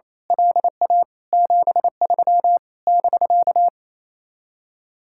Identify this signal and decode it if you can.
Morse code, heard as a keyed tone, about 26 wpm, 700 Hz. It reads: LA 73 <BK>
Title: Speed-Racing Common QSO Phrases Encore - 28wpm